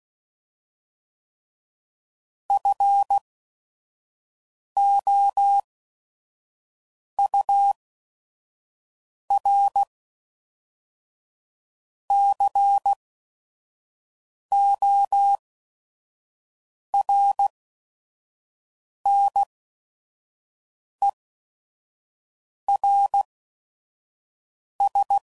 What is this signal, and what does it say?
Morse code, heard as a keyed tone, about 16 wpm, spaced as for 5 wpm, 800 Hz. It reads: FOURCORNERS
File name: CW Code Groups @5WPM / fromSM's / Cities